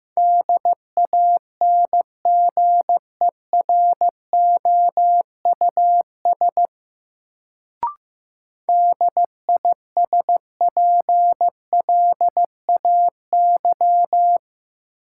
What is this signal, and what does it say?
Morse code, heard as a keyed tone, about 15 wpm, 700 Hz. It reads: DANGEROUS DISPLAY
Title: Sentences from Top 1200 Words - New Word Drill - 15wpm